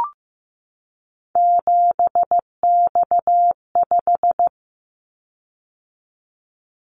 Morse code, heard as a keyed tone, about 15 wpm, 700 Hz. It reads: 7X5
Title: Callsign Prefixes for Top 100 DXCC - 15wpm